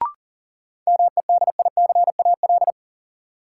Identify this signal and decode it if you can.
Morse code, heard as a keyed tone, about 40 wpm, 700 Hz. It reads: MEDICAL
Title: Sentences from Top 800 Words - New Word Drill - 40wpm